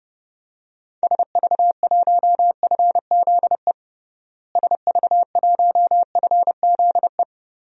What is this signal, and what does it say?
Morse code, heard as a keyed tone, about 30 wpm, 700 Hz. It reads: SV1FZE SV1FZE
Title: DXCC-100 Call Signs - 6 Characters - Repeated 3x - 30wpm